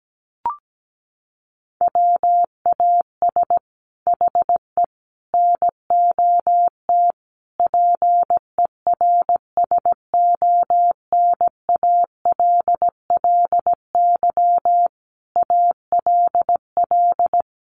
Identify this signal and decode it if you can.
Morse code, heard as a keyed tone, about 17 wpm, 700 Hz. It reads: WAS HE NOT PERSONALLY ALL
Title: Sentences from Top 1400 Words 17wpm